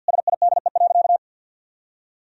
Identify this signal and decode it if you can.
Morse code, heard as a keyed tone, about 50 wpm, 700 Hz. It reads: SIDE.
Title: Sentences from Top 1000 Words - Review Part 4 - 50wpm